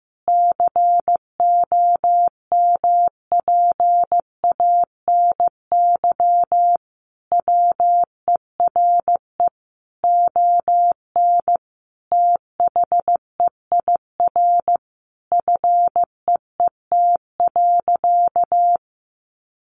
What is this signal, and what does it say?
Morse code, heard as a keyed tone, about 15 wpm, 700 Hz. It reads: COMPANY WERE ON THEIR FEET.
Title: Sentences from Top 1000 Words 15wpm